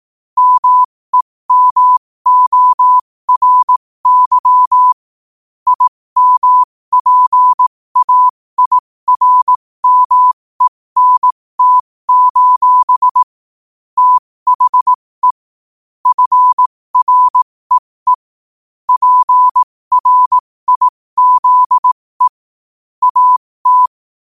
Morse code, heard as a keyed tone, about 18 wpm, spaced as for 15 wpm, 1000 Hz. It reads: MEMORY IMPAIRMENT: THE FREE PRIZE AT
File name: qotdfiles / 15wpm